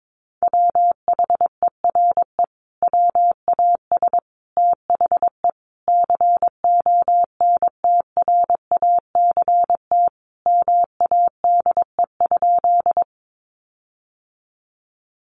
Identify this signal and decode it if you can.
Morse code, heard as a keyed tone, about 22 wpm, 700 Hz. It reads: WHERE WAS THE CONTRACT MADE?
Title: Sentences from Top 1400 Words 22wpm